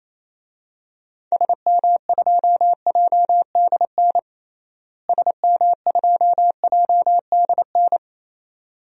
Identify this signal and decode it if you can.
Morse code, heard as a keyed tone, about 28 wpm, 700 Hz. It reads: SM2JDN SM2JDN
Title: DXCC-100 Call Signs - 6 Characters - Repeated 2x - 28wpm